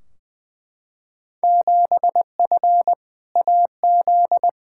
Morse code, heard as a keyed tone, about 20 wpm, 700 Hz. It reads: 7F AZ
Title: Field Day Exchanges Encore - 20wpm